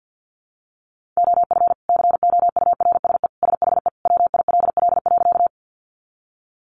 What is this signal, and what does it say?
Morse code, heard as a keyed tone, about 50 wpm, 700 Hz. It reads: OF COURSE SHE WILL.
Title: Sentences from Top 500 Words 50wpm